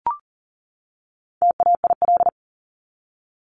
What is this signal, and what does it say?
Morse code, heard as a keyed tone, about 40 wpm, 700 Hz. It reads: TAIL
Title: Top 500 Words 40wpm